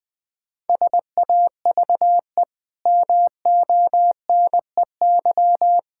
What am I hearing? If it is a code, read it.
Morse code, heard as a keyed tone, about 20 wpm, 700 Hz. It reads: SAVE MONEY
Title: Easy Sentences with Hints - 20wpm